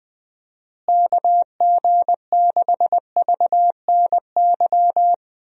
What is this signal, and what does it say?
Morse code, heard as a keyed tone, about 20 wpm, 700 Hz. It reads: KG6VNY
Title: USA Call Signs - 6 Characters - Repeated 3x - 20wpm